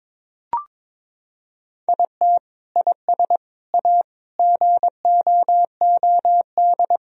Morse code, heard as a keyed tone, about 21 wpm, 700 Hz. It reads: IT IS A GOOD
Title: Sentences from Top 1000 Words - Review Part 3 - 22wpm